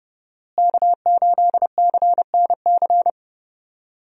Morse code, heard as a keyed tone, about 30 wpm, 700 Hz. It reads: K8CNC
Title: USA Call Signs - 5 Characters - Repeated 3x - 30wpm